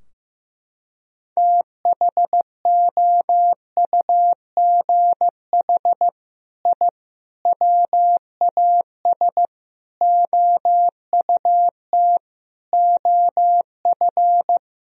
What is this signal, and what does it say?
Morse code, heard as a keyed tone, about 15 wpm, 700 Hz. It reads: THOUGH I WAS OUT OF
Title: Sentences from Top 1000 Words - Review Part 4 - 15wpm